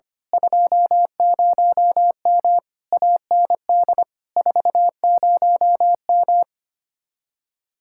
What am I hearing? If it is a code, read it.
Morse code, heard as a keyed tone, about 24 wpm, 700 Hz. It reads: 20M AND 40M
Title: Speed-Racing Common QSO Phrases - 25wpm